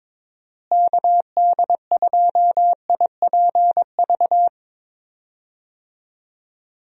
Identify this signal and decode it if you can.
Morse code, heard as a keyed tone, about 21 wpm, 700 Hz. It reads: KD2IPV
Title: USA Call Signs - 6 Characters - Repeated 3x - 22wpm